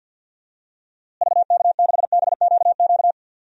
Morse code, heard as a keyed tone, about 53 wpm, 700 Hz. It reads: VK6BQQ
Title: DXCC-100 Call Signs - 6 Characters - Repeated 2x - 50wpm